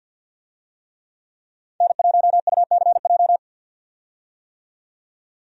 Morse code, heard as a keyed tone, about 48 wpm, 700 Hz. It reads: N1UCJ
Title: USA Call Signs - 5 Characters - 50wpm